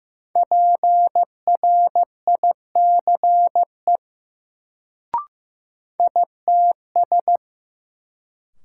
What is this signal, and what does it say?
Morse code, heard as a keyed tone, about 15 wpm, 700 Hz. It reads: PRICE ITS
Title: Sentences from Top 600 Words - New Word Drill - 15wpm